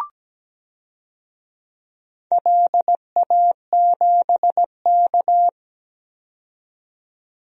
Morse code, heard as a keyed tone, about 17 wpm, 700 Hz. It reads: LA7K
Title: DXCC-100 Call Signs - 4 Characters - 17wpm